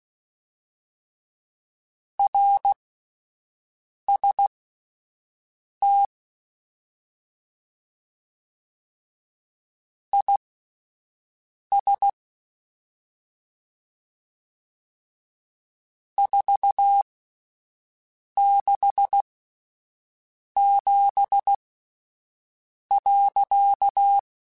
Morse code, heard as a keyed tone, about 16 wpm, spaced as for 5 wpm, 800 Hz. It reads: RST IS 467.
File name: (spaced) / QSOs Spaced-mp3s / kc7haa @10 sp